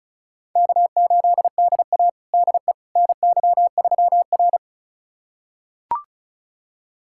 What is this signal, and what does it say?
Morse code, heard as a keyed tone, about 36 wpm, 700 Hz. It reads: K8DA DE NY3R
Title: USA Callsign DE Callsign - 4 Characters - 35wpm